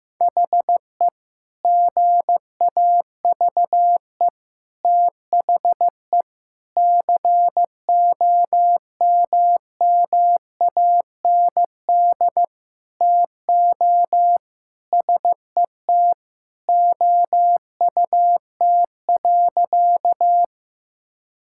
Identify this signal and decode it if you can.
Morse code, heard as a keyed tone, about 15 wpm, 700 Hz. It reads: HE GAVE THE COMMAND TO SET OUT.
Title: Sentences from Top 1600 Words 15wpm